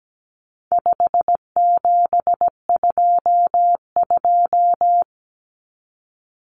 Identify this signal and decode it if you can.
Morse code, heard as a keyed tone, about 17 wpm, 700 Hz. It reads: 5722
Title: Common Cut Numbers - 4-Digit 17wpm